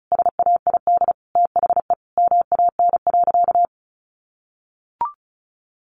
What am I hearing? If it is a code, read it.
Morse code, heard as a keyed tone, about 34 wpm, 700 Hz. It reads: SAID THE MAN.
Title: Sentences from Top 400 Words 35wpm